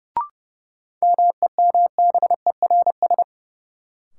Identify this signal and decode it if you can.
Morse code, heard as a keyed tone, about 30 wpm, 700 Hz. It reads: MEMBERS
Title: Sets of 3 Words - New Word Drill - 30wpm